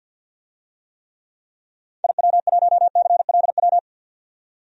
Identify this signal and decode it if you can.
Morse code, heard as a keyed tone, about 49 wpm, 700 Hz. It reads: IW1CLW
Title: DXCC-100 Call Signs - 6 Characters - 50wpm